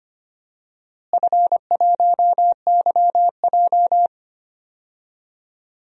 Morse code, heard as a keyed tone, about 25 wpm, 700 Hz. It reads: F1YJ
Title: DXCC-100 Call Signs - 4 Characters - Repeated 2x - 25wpm